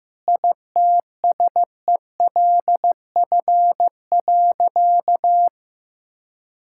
Morse code, heard as a keyed tone, about 16 wpm, 700 Hz. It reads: ITSELF.
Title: Sentences from Top 1000 Words - Review Part 5 - 15wpm